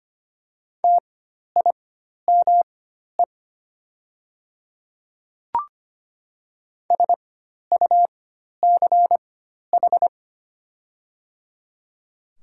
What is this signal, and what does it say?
Morse code, heard as a keyed tone, about 25 wpm, spaced as for 12 wpm, 700 Hz. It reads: TIME SUCH
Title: 4x Character Spacing - Top 200 Words - 25wpm